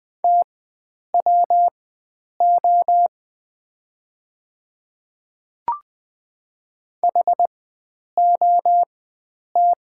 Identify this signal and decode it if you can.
Morse code, heard as a keyed tone, about 20 wpm, spaced as for 9 wpm, 700 Hz. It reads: TWO HOT
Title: 4x Character Spacing - Top 100 Words - 20wpm